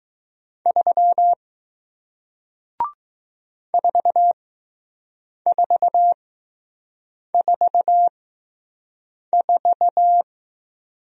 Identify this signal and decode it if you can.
Morse code, heard as a keyed tone, about 20 wpm, spaced as for 6 wpm, 700 Hz. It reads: 3 4444